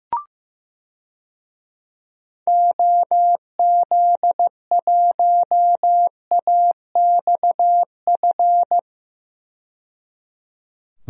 Morse code, heard as a keyed tone, about 15 wpm, 700 Hz. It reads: OZ1AXF
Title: DXCC-100 Call Signs - 6 Characters - 15wpm